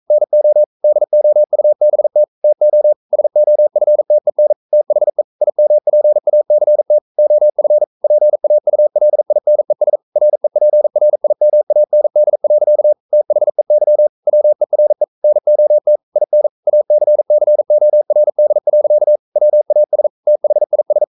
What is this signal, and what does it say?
Morse code, heard as a keyed tone, about 42 wpm, 600 Hz. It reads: NO DOUBT TO SOFTEN THE IMPACT OF PAULINES REPRIMAND. THEY WERE NOT IN ACCORD. WAS THIS
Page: Romanen Never af Ken Follet på engelsk.